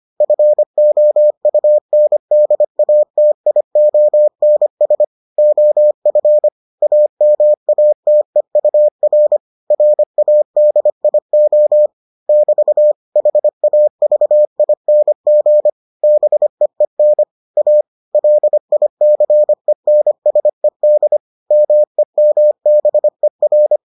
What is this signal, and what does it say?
Morse code, heard as a keyed tone, about 25 wpm, 600 Hz. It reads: FOUNDATIONS OF AMATEUR RADIO <BT> HAVING BEEN A LICENSED MEMBER